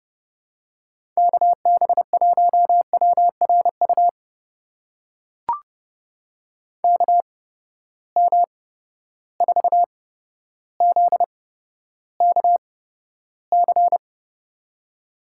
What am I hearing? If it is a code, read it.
Morse code, heard as a keyed tone, about 31 wpm, spaced as for 13 wpm, 700 Hz. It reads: KB1WRU KM4ZKC
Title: USA Call Signs - 6 Characters - 8x Character Spacing - 30wpm